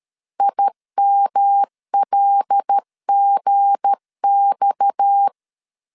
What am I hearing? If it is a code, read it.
Morse code, heard as a keyed tone, about 13 wpm, 800 Hz. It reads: IMLGX
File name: LEZIONI / 7 GF9Z / 7 GF9 12wpm 2319